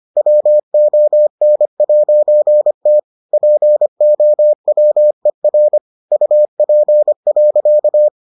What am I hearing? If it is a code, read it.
Morse code, heard as a keyed tone, about 25 wpm, 600 Hz. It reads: WON'T POWER UP.